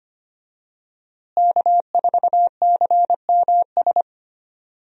Morse code, heard as a keyed tone, about 25 wpm, 700 Hz. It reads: K4CMS
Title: USA Call Signs - 5 Characters - Repeated 2x - 25wpm